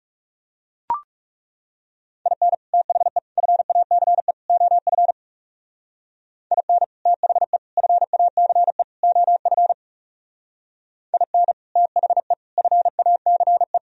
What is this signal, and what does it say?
Morse code, heard as a keyed tone, about 35 wpm, 700 Hz. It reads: IN THE FACE OF IN THE FACE OF IN THE FACE
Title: Speed-Racing - Sets of 4 Words 30wpm